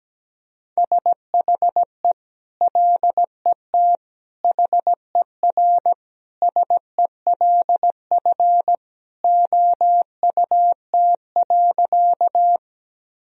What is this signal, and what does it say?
Morse code, heard as a keyed tone, about 17 wpm, 700 Hz. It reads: SHE LET HERSELF OUT.
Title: Sentences from Top 1000 Words - Review Part 1 - 17wpm